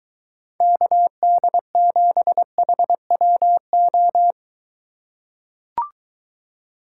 Morse code, heard as a keyed tone, about 23 wpm, 700 Hz. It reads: KD7HWO